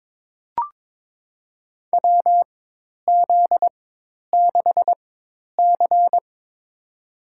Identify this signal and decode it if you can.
Morse code, heard as a keyed tone, about 22 wpm, spaced as for 10 wpm, 700 Hz. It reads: WZ6C
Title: USA Call Signs - 4 Characters - 4x Character Spacing - 22wpm